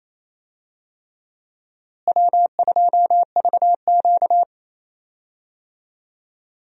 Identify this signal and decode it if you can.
Morse code, heard as a keyed tone, about 28 wpm, 700 Hz. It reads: W2VQ
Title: USA Call Signs - 4 Characters - Repeated 3x - 28wpm